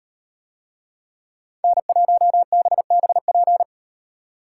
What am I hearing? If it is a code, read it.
Morse code, heard as a keyed tone, about 38 wpm, 700 Hz. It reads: N1BBP